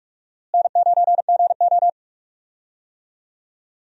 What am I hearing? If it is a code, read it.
Morse code, heard as a keyed tone, about 46 wpm, 700 Hz. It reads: N9GO
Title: USA Call Signs - 4 Characters - Repeated 3x - 45wpm